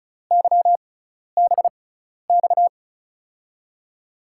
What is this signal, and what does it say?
Morse code, heard as a keyed tone, about 36 wpm, spaced as for 12 wpm, 700 Hz. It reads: YBX
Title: USA Call Signs - 6 Characters - 6x Character Spacing - 35wpm